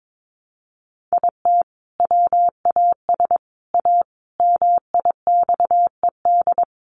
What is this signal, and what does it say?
Morse code, heard as a keyed tone, about 22 wpm, 700 Hz. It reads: IT WAS A MIXED
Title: Sentences from Top 1500 Words 22wpm